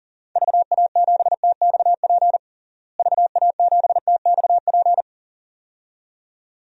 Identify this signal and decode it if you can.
Morse code, heard as a keyed tone, about 40 wpm, 700 Hz. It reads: VA7TXP VA7TXP
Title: DXCC-100 Call Signs - 6 Characters - Repeated 2x - 40wpm